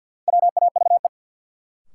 Morse code, heard as a keyed tone, about 48 wpm, 700 Hz. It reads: WAVE
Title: Sentences from Top 1400 Words - New Word Drill Alt - 50wpm